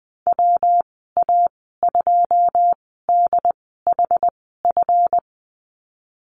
Morse code, heard as a keyed tone, about 20 wpm, spaced as for 15 wpm, 700 Hz. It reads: WA2DHF
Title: 179 Lesson 29g - Slash - Call Signs 20wpm